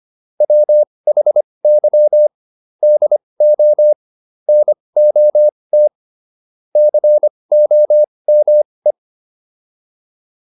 Morse code, heard as a keyed tone, about 25 wpm, spaced as for 20 wpm, 600 Hz. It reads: WHY DO NOT COME